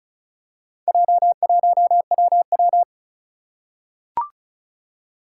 Morse code, heard as a keyed tone, about 36 wpm, 700 Hz. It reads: J1WW
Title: Four Challenging Letter-Number Combinations 35wpm